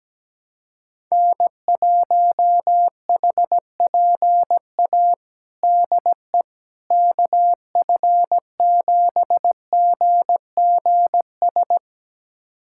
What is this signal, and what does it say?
Morse code, heard as a keyed tone, about 17 wpm, 700 Hz. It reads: N1HPA DE KF7GGS
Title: USA Callsign DE Callsign 17wpm